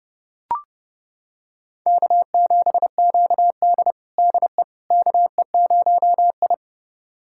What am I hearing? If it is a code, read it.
Morse code, heard as a keyed tone, about 32 wpm, 700 Hz. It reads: K7QD DE KE0I
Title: USA Callsign DE Callsign - 4 Characters - 30wpm